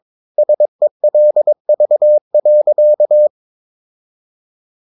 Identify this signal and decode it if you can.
Morse code, heard as a keyed tone, about 22 wpm, 600 Hz. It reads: SELV.
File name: Ferie QRS 22wpm